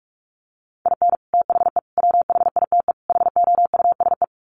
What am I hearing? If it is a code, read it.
Morse code, heard as a keyed tone, about 44 wpm, 700 Hz. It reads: IN THE WHITE HOUSE
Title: Sets of 4 Words 45wpm